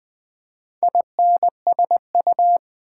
Morse code, heard as a keyed tone, about 20 wpm, 700 Hz. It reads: INSU
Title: Build A Word - 4 Letters At A Time - 12-Letter Words - 20wpm